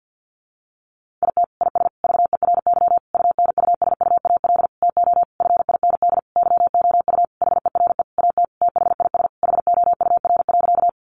Morse code, heard as a keyed tone, about 50 wpm, 700 Hz. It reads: IT IS VERY UNUSUAL TO FIND YOU HERE AT THIS HOUR.
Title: Sentences from Top 1400 Words 50wpm